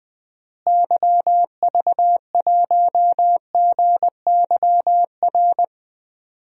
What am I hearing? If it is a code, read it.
Morse code, heard as a keyed tone, about 20 wpm, 700 Hz. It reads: YV1GYR
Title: DXCC-100 Call Signs - 6 Characters - Repeated 2x - 20wpm